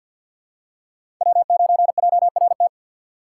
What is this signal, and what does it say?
Morse code, heard as a keyed tone, about 52 wpm, 700 Hz. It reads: W9JRT
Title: USA Call Signs - 5 Characters - 3x Character Spacing - 50wpm